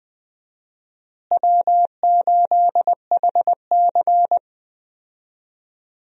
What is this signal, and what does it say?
Morse code, heard as a keyed tone, about 20 wpm, 700 Hz. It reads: W8HC
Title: USA Call Signs - 4 Characters - Repeated 2x - 20wpm